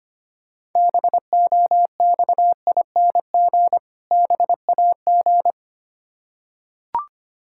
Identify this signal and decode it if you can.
Morse code, heard as a keyed tone, about 25 wpm, 690 Hz. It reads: BOXING BAG
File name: practice / 25wpm / Difficult Words with Hints - 25wpm